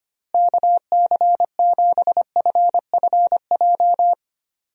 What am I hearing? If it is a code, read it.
Morse code, heard as a keyed tone, about 25 wpm, 700 Hz. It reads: KC7FFJ
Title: USA Call Signs - 6 Characters - Repeated 2x - 25wpm